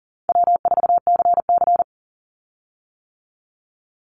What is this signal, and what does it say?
Morse code, heard as a keyed tone, about 40 wpm, 700 Hz. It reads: W4CC
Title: Easy Call Signs - 40wpm